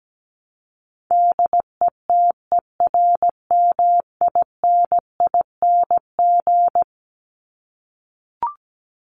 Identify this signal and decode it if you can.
Morse code, heard as a keyed tone, about 17 wpm, 700 Hz. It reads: DETERMINING
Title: ING Suffix - Top 100 Words with Base Words 17wpm